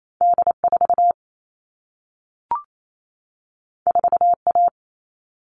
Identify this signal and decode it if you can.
Morse code, heard as a keyed tone, about 28 wpm, 700 Hz. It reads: D4 4A
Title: All Cut Numbers - 2-Digit 28wpm